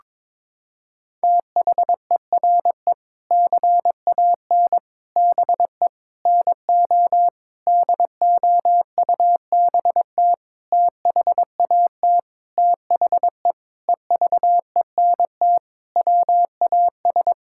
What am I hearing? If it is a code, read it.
Morse code, heard as a keyed tone, about 22 wpm, 700 Hz. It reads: THERE CAN BE NO DOUBT THAT THE EVENT WAS
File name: Sentences with 7-Word Phrases - 22wpm